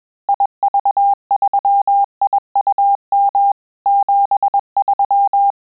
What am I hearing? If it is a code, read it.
Morse code, heard as a keyed tone, about 21 wpm, 800 Hz. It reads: IV3IUM 73
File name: Telegraphy-group / breve testo midi